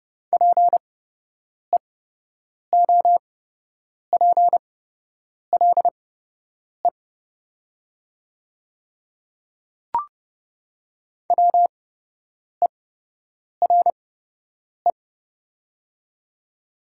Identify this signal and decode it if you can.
Morse code, heard as a keyed tone, about 30 wpm, spaced as for 8 wpm, 700 Hz. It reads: PEOPLE WERE
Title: 8x Character Spacing - Top 100 Words - 30wpm